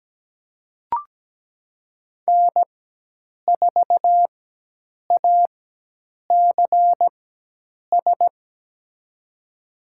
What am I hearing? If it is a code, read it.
Morse code, heard as a keyed tone, about 17 wpm, spaced as for 8 wpm, 700 Hz. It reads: N4ACS
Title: USA Call Signs - 5 Characters - 4x Character Spacing - 17wpm